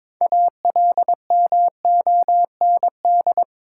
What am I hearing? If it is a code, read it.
Morse code, heard as a keyed tone, about 21 wpm, 690 Hz. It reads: ALMOND
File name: Difficult Words with Hints - 22wpm